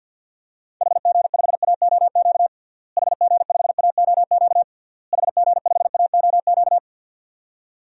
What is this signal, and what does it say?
Morse code, heard as a keyed tone, about 51 wpm, 700 Hz. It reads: HG5AOQ HG5AOQ HG5AOQ
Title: DXCC-100 Call Signs - 6 Characters - Repeated 3x - 50wpm